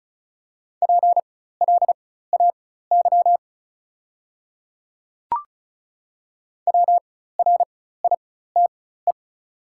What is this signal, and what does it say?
Morse code, heard as a keyed tone, about 34 wpm, spaced as for 16 wpm, 700 Hz. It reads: PLAY WRITE
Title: 4x Character Spacing - Top 300 Words - 35wpm